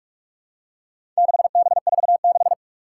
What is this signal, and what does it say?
Morse code, heard as a keyed tone, about 46 wpm, 700 Hz. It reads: 6B46
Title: Four Challenging Letter-Number Combinations 45wpm